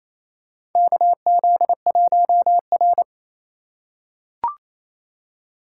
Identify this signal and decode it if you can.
Morse code, heard as a keyed tone, about 28 wpm, 700 Hz. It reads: KZ1R
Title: USA Call Signs - 4 Characters - 2x Character Spacing - 28wpm